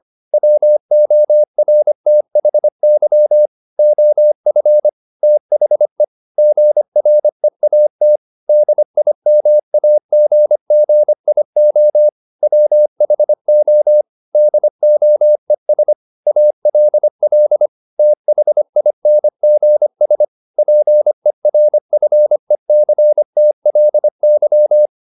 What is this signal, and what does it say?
Morse code, heard as a keyed tone, about 25 wpm, 600 Hz. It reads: WORTHY OF THE GREAT DIMAGGIO WHO DOES ALL THINGS PERFECTLY
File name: the old man and the sea part 1 25wpm